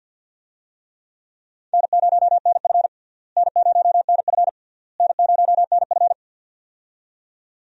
Morse code, heard as a keyed tone, about 47 wpm, 700 Hz. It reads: N0NF N0NF N0NF
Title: USA Call Signs - 4 Characters - Repeated 3x - 50wpm